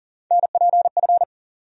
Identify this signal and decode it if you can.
Morse code, heard as a keyed tone, about 40 wpm, 700 Hz. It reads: NPF
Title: USA Call Sign Postfixes - 3 Letters 40wpm